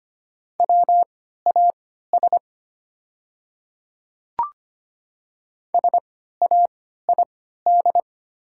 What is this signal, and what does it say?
Morse code, heard as a keyed tone, about 25 wpm, spaced as for 14 wpm, 700 Hz. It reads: WAS SAID
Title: 3x Character Spacing - Top 500 Words - 25wpm